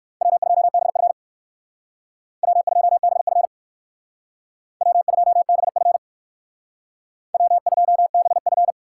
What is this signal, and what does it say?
Morse code, heard as a keyed tone, about 58 wpm, 700 Hz. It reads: W2BF W2BF W2BF W2BF